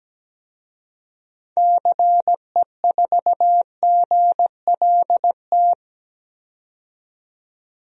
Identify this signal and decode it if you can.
Morse code, heard as a keyed tone, about 17 wpm, 700 Hz. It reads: CE4GLT
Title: DXCC-100 Call Signs - 6 Characters - 17wpm